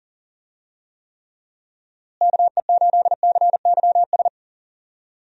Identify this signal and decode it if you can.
Morse code, heard as a keyed tone, about 40 wpm, 700 Hz. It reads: KE8CYS
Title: USA Call Signs - 6 Characters - Repeated 2x - 40wpm